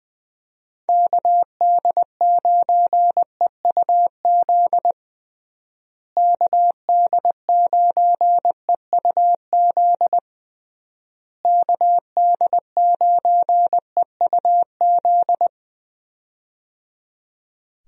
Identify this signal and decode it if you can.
Morse code, heard as a keyed tone, about 21 wpm, 700 Hz. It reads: KD9EUZ KD9EUZ KD9EUZ
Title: USA Call Signs - 6 Characters - Repeated 3x - 20wpm